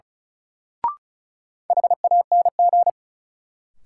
Morse code, heard as a keyed tone, about 36 wpm, 700 Hz. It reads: HANG
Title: Sentences from Top 1600 Words - New Word Drill - 35wpm